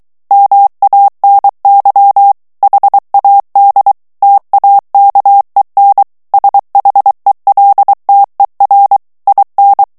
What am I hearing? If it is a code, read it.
Morse code, heard as a keyed tone, about 23 wpm, 800 Hz. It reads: MANY HAD TAKEN SHELTER IN
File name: morse mp3 / ANGLAIS texte clair 2 vitesse 22 mots minute